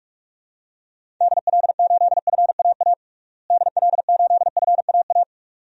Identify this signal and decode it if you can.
Morse code, heard as a keyed tone, about 42 wpm, 700 Hz. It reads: DL8FAA DL8FAA
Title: DXCC-100 Call Signs - 6 Characters - Repeated 2x - 45wpm